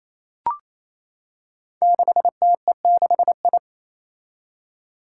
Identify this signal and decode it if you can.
Morse code, heard as a keyed tone, about 27 wpm, 700 Hz. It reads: 6TE6I
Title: Five Challenging Letter-Number Combinations 28wpm